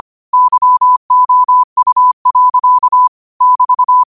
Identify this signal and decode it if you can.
Morse code, heard as a keyed tone, about 25 wpm, 1000 Hz. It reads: YOU. <BT>
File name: qotdfiles / 25wpm